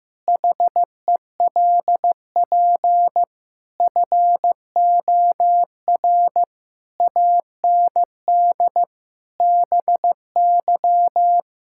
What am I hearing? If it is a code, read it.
Morse code, heard as a keyed tone, about 15 wpm, 700 Hz. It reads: HELP FOR AND BY
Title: Pangram Challenge v2 - 15wpm